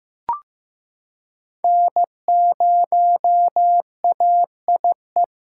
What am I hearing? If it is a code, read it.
Morse code, heard as a keyed tone, about 15 wpm, 700 Hz. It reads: N0AIE
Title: K1USN SST QSO Practice - 15wpm